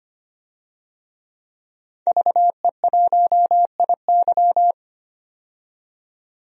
Morse code, heard as a keyed tone, about 25 wpm, 700 Hz. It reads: VE1IY
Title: DXCC-100 Call Signs - 5 Characters - 25wpm